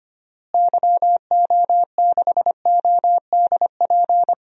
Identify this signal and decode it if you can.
Morse code, heard as a keyed tone, about 25 wpm, 700 Hz. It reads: YO6ODP
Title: DXCC-100 Call Signs - 6 Characters - Repeated 2x - 25wpm